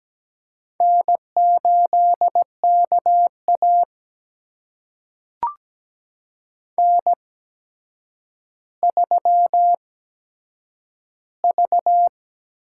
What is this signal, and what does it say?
Morse code, heard as a keyed tone, about 17 wpm, spaced as for 8 wpm, 700 Hz. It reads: N8KA N3V
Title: USA Call Signs - 4 Characters - 8x Character Spacing - 17wpm